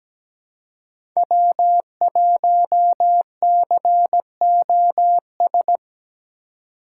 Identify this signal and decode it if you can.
Morse code, heard as a keyed tone, about 17 wpm, 700 Hz. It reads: W1COS
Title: USA Call Signs - 5 Characters - 4x Character Spacing - 17wpm